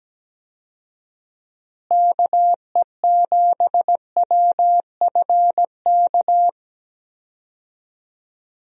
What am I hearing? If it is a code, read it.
Morse code, heard as a keyed tone, about 17 wpm, 700 Hz. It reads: KE7WFK
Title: USA Call Signs - 6 Characters - Repeated 2x - 17wpm